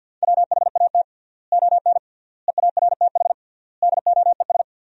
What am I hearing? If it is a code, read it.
Morse code, heard as a keyed tone, about 53 wpm, 700 Hz. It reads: WHAT ON EARTH DOES
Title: 3x Word-Spacing - Sentences from Top 200 Words - 50wpm